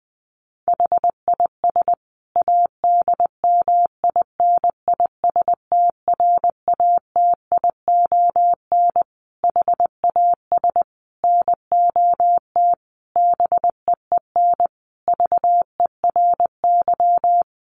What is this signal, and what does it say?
Morse code, heard as a keyed tone, about 20 wpm, 700 Hz. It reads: HIS ADMINISTRATION HAS NOT BEEN VERY
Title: Sentences from Top 1300 Words 20wpm